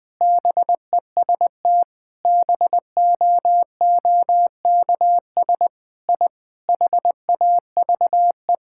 Morse code, heard as a keyed tone, about 21 wpm, 700 Hz. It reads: BEST BOOKS I HAVE
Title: Sentences with 5-Word Phrases - 20wpm